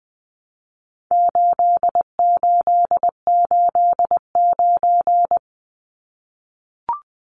Morse code, heard as a keyed tone, about 20 wpm, 700 Hz. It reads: 8889
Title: Numbers - 4-Digit 20wpm